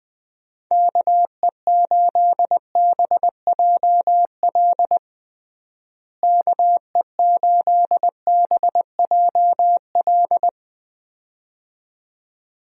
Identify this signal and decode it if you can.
Morse code, heard as a keyed tone, about 21 wpm, 700 Hz. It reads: KE8BJL KE8BJL
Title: USA Call Signs - 6 Characters - Repeated 2x - 20wpm